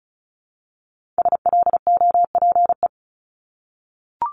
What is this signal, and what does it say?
Morse code, heard as a keyed tone, about 36 wpm, 700 Hz. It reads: SLOPE
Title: Multi-letter Phonemes 35wpm